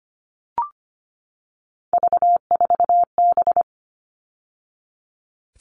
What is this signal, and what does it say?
Morse code, heard as a keyed tone, about 25 wpm, 700 Hz. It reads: V4B
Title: All Cut Numbers - 3-Digit 25wpm